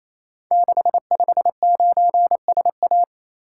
Morse code, heard as a keyed tone, about 28 wpm, 700 Hz. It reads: 659SA
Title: Code Groups - 5 Letter:Numbers - 28wpm